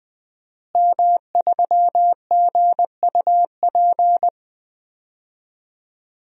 Morse code, heard as a keyed tone, about 20 wpm, 700 Hz. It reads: M3GUP
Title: DXCC-100 Call Signs - 5 Characters - Repeated 2x - 20wpm